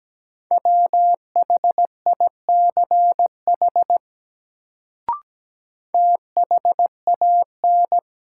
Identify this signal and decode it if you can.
Morse code, heard as a keyed tone, about 17 wpm, 700 Hz. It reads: WHICH THAN
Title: Sentences from Top 100 Words - New Word Drill - 17wpm